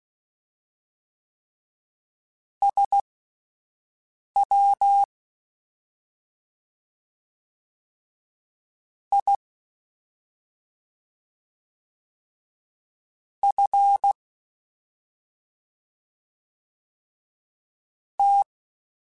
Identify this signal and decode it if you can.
Morse code, heard as a keyed tone, about 16 wpm, spaced as for 5 wpm, 800 Hz. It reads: SW I F T
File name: Misc / Misc-mp3s / Jumps @5